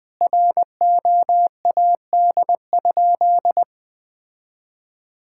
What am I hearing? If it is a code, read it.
Morse code, heard as a keyed tone, about 20 wpm, 700 Hz. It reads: ROAD?
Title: Sentences from Top 1000 Words - Review Part 5 - 20wpm